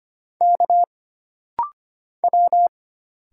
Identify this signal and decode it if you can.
Morse code, heard as a keyed tone, about 25 wpm, 700 Hz. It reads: K W
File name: Single Character (Letter:Number:Punctuation:Prosign) - Rapid-Fire - 25wpm